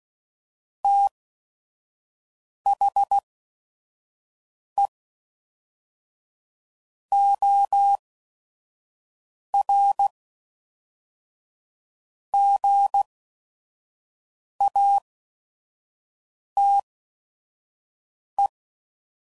Morse code, heard as a keyed tone, about 16 wpm, spaced as for 5 wpm, 800 Hz. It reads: THE OR GATE
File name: CW Code Groups @5WPM / Technical Subjects / OR Gate